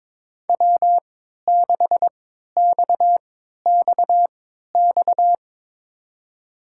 Morse code, heard as a keyed tone, about 22 wpm, spaced as for 13 wpm, 700 Hz. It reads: W6XXX
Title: USA Call Signs - 5 Characters - 3x Character Spacing - 22wpm